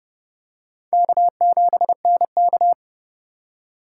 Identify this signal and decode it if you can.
Morse code, heard as a keyed tone, about 30 wpm, 700 Hz. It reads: K7NK
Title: USA Call Signs - 4 Characters - 2x Character Spacing - 30wpm